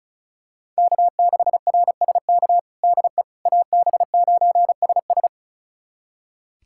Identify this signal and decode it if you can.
Morse code, heard as a keyed tone, about 34 wpm, 700 Hz. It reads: K6RSK DE AB9SS
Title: USA Callsign DE Callsign - 5 Characters - 35wpm